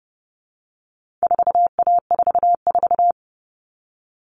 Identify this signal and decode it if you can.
Morse code, heard as a keyed tone, about 30 wpm, 700 Hz. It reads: 4A44
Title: All Cut Numbers - 4-Digit 30wpm